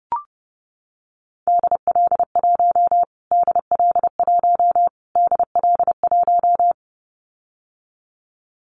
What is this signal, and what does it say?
Morse code, heard as a keyed tone, about 30 wpm, 700 Hz. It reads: DL1 DL1 DL1
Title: Callsign Prefixes for Top 25 DXCC - Repeated 3x - 30wpm